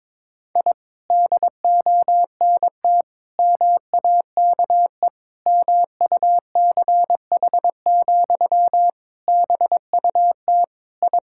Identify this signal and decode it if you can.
Morse code, heard as a keyed tone, about 22 wpm, 700 Hz. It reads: I DONT MAKE MUCH, BUT I
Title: Sentences from Top 1000 Words - Review Part 5 - 22wpm